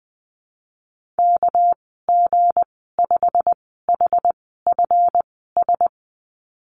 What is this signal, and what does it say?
Morse code, heard as a keyed tone, about 19 wpm, spaced as for 14 wpm, 700 Hz. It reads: KG5HFS
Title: 213 Lesson 34g - Number 8 - Call Signs 20wpm